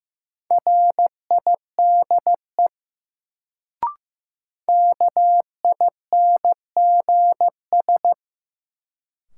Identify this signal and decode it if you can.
Morse code, heard as a keyed tone, about 15 wpm, 700 Hz. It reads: RIDE KINGS
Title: Sentences from Top 1000 Words - Review Part 6 - New Word Drill - 15wpm